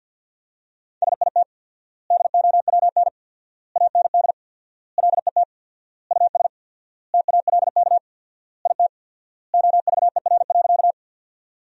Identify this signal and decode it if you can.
Morse code, heard as a keyed tone, about 50 wpm, 700 Hz. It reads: SIT DOWN AND LET US TALK IT OVER.
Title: 4x Word-Spacing - Sentences from Top 500 Words - 50wpm